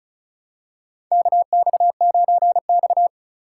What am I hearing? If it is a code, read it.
Morse code, heard as a keyed tone, about 36 wpm, 700 Hz. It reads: KX9X
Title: USA Call Signs - 4 Characters - 2x Character Spacing - 35wpm